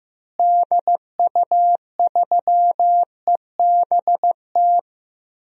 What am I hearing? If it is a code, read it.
Morse code, heard as a keyed tone, about 16 wpm, 700 Hz. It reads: DU3EBT
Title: DXCC-100 Call Signs - 6 Characters - Repeated 2x - 15wpm